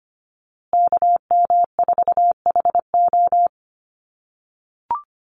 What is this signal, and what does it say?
Morse code, heard as a keyed tone, about 25 wpm, 700 Hz. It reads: KM4HO
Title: USA Call Signs Repeated 3x 25wpm